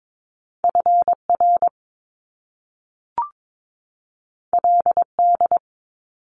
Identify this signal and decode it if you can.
Morse code, heard as a keyed tone, about 22 wpm, 700 Hz. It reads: FR LD
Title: Top Two Letter Combinations in Words 22wpm